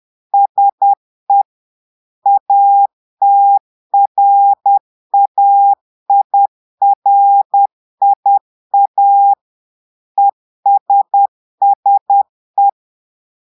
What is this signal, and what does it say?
Morse code, heard as a keyed tone, about 10 wpm, 800 Hz. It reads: SE ATRAIRIA ESSE
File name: portugais texte en clair 2 vitesse 10 mots minutes